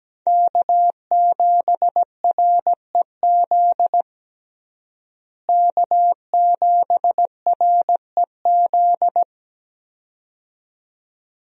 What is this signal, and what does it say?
Morse code, heard as a keyed tone, about 17 wpm, 700 Hz. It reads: K7REZ K7REZ
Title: USA Call Signs - 5 Characters - Repeated 2x - 17wpm